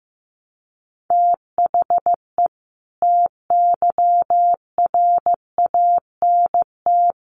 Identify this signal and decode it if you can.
Morse code, heard as a keyed tone, about 15 wpm, 700 Hz. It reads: THE TYRANT
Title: A Sound Of Thunder 15wpm